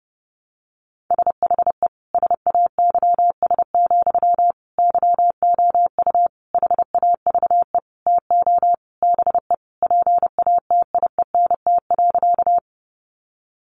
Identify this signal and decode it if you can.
Morse code, heard as a keyed tone, about 30 wpm, 700 Hz. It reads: SHE SAYS, YOU HAVE TO BE PATIENT.
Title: Easy Phrases 30wpm